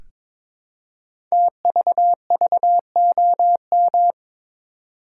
Morse code, heard as a keyed tone, about 22 wpm, 700 Hz. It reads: TVVOM
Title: Code Groups - 5 Letters - 22wpm